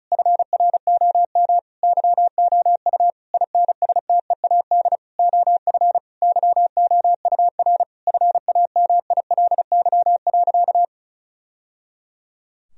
Morse code, heard as a keyed tone, about 36 wpm, 700 Hz. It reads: FROM YOU INSTEAD OF YOUR FAMILY.
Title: Sentences from Top 1000 Words - Review Part 4 - 35wpm